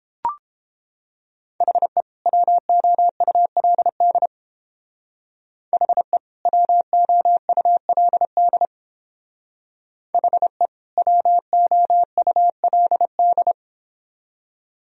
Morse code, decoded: HE WOULD HE WOULD HE WOULD

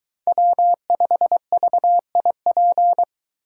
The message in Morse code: W5VIP

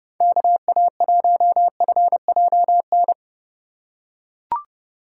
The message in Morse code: KA1FJN

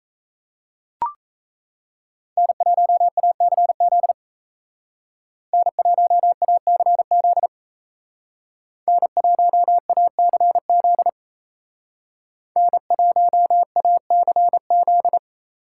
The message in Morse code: N1ACZ N1ACZ N1ACZ N1ACZ

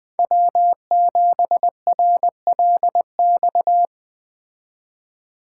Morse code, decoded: W7RLX

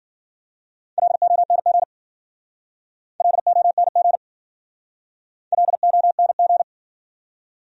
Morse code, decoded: LONG LONG LONG

